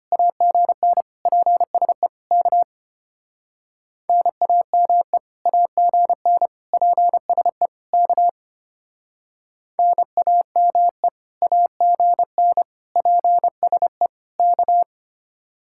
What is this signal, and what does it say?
Morse code, heard as a keyed tone, about 31 wpm, 700 Hz. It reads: AGN PSE K NAME AGN PSE K NAME AGN PSE K